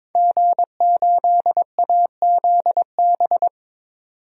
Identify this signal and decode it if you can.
Morse code, heard as a keyed tone, about 22 wpm, 700 Hz. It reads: G8AZB
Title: DXCC-100 Call Signs - 5 Characters - 22wpm